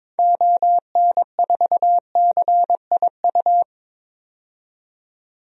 Morse code, heard as a keyed tone, about 21 wpm, 700 Hz. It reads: ON4CIU
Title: DXCC-100 Call Signs - 6 Characters - 22wpm